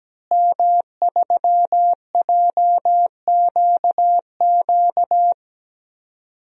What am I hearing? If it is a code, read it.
Morse code, heard as a keyed tone, about 17 wpm, 700 Hz. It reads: M3JQQ
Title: DXCC-100 Call Signs - 5 Characters - Repeated 3x - 17wpm